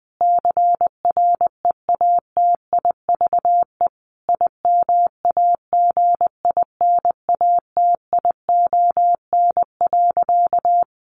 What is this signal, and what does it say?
Morse code, heard as a keyed tone, about 19 wpm, 700 Hz. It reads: CREATIVE IMAGINATION.
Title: Sentences from Top 1300 Words 20wpm